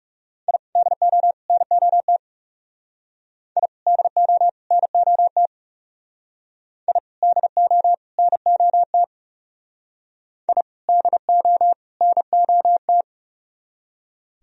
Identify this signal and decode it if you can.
Morse code, heard as a keyed tone, about 39 wpm, 700 Hz. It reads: I DO NOT I DO NOT I DO NOT I DO NOT